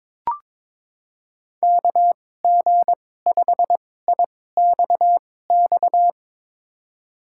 Morse code, heard as a keyed tone, about 22 wpm, spaced as for 16 wpm, 700 Hz. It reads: KG5IXX
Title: USA Call Signs - 2x Character Spacing - 22wpm